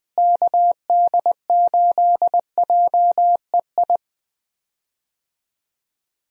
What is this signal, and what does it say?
Morse code, heard as a keyed tone, about 20 wpm, 700 Hz. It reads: KD8JEI
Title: USA Call Signs - 6 Characters - Repeated 3x - 20wpm